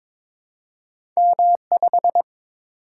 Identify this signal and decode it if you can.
Morse code, heard as a keyed tone, about 22 wpm, 700 Hz. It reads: M5
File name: practice / 22wpm / Two Letter-Number Combinations - 22wpm